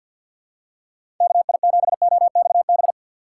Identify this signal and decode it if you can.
Morse code, heard as a keyed tone, about 47 wpm, 700 Hz. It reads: KI7OXB
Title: USA Call Signs - 4x Character Spacing - 50wpm